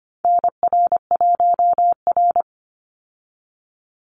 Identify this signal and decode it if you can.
Morse code, heard as a keyed tone, about 25 wpm, 700 Hz. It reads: NR1R
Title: Call Signs Repeated 2x - 25wpm